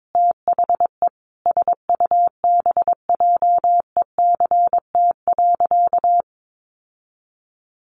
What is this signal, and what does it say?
Morse code, heard as a keyed tone, about 22 wpm, 700 Hz. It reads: THE SUBJECT.
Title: Sentences from Top 1400 Words 22wpm